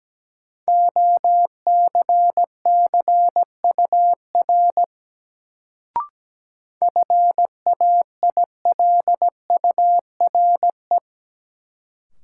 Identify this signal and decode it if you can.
Morse code, heard as a keyed tone, about 17 wpm, 700 Hz. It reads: OCCUR FAILURE
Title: Sentences from Top 900 Words - New Word Drill - 17wpm